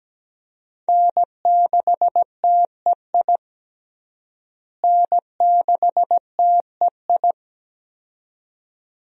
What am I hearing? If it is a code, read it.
Morse code, heard as a keyed tone, about 17 wpm, 700 Hz. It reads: N6TEI N6TEI
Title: USA Call Signs - 5 Characters - Repeated 2x - 17wpm